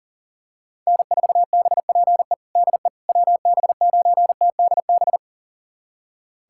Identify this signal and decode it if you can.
Morse code, heard as a keyed tone, about 40 wpm, 700 Hz. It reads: N4BPE DE WB9TDB
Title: USA Callsign DE Callsign 40wpm